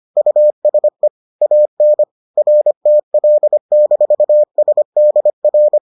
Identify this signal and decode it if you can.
Morse code, heard as a keyed tone, about 25 wpm, 600 Hz. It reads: USE AN RTL-SDR